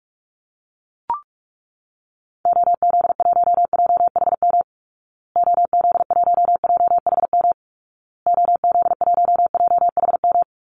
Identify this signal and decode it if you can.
Morse code, heard as a keyed tone, about 44 wpm, 700 Hz. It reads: OZ1JHM OZ1JHM OZ1JHM
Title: Call Signs Repeated 2x - 45wpm